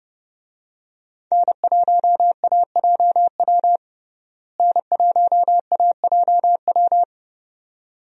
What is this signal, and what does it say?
Morse code, heard as a keyed tone, about 30 wpm, 700 Hz. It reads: N1AJW N1AJW
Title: USA Call Signs - 5 Characters - Repeated 2x - 30wpm